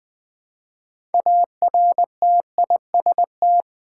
Morse code, heard as a keyed tone, about 20 wpm, 700 Hz. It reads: ARTIST
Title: 6 Letter Words - First 3 Letters Given - 20wpm